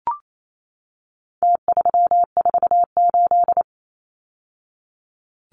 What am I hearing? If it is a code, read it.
Morse code, heard as a keyed tone, about 28 wpm, 700 Hz. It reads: T348
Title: Numbers - Contest Style - 4-Digit 28wpm